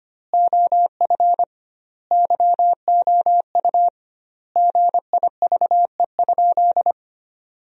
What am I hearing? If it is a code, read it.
Morse code, heard as a keyed tone, about 25 wpm, 700 Hz. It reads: OF YOU GIVE?
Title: 2x Word-Spacing - Sentences from Top 200 Words - 25wpm